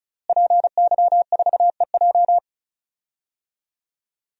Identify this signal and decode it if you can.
Morse code, heard as a keyed tone, about 34 wpm, 700 Hz. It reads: PY4EJ
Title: DXCC-100 Call Signs - 5 Characters - Repeated 2x - 35wpm